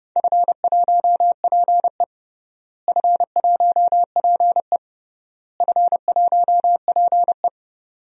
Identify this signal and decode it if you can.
Morse code, heard as a keyed tone, about 31 wpm, 700 Hz. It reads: F1PE F1PE F1PE
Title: DXCC-100 Call Signs - 4 Characters - Repeated 3x - 30wpm